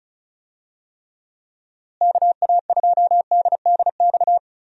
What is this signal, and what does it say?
Morse code, heard as a keyed tone, about 36 wpm, 700 Hz. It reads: KA2DDX
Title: USA Call Signs - 6 Characters - Repeated 2x - 35wpm